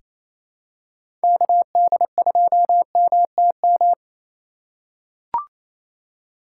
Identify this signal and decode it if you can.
Morse code, heard as a keyed tone, about 28 wpm, 700 Hz. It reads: KD2MTM
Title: USA Call Signs - 6 Characters - 2x Character Spacing - 28wpm